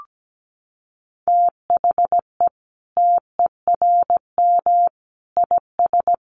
Morse code, heard as a keyed tone, about 17 wpm, 700 Hz. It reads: THE TERM IS
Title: Sentences from Top 900 Words 17wpm